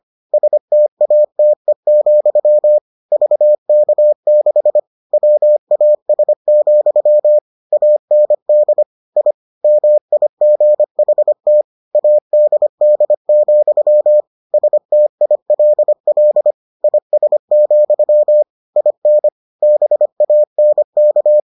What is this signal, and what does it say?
Morse code, heard as a keyed tone, about 25 wpm, 600 Hz. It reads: STATE, VK6 WAS, AND I MIGHT ADD, STILL IS, IN BANK